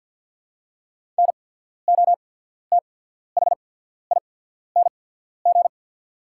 Morse code, heard as a keyed tone, about 48 wpm, spaced as for 14 wpm, 700 Hz. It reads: NOTHING
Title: 8x Character Spacing - Top 500 Words - 50wpm